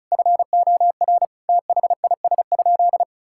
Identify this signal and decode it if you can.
Morse code, heard as a keyed tone, about 36 wpm, 700 Hz. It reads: FOR THIS?
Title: Sentences from Top 1000 Words - Review Part 5 - 35wpm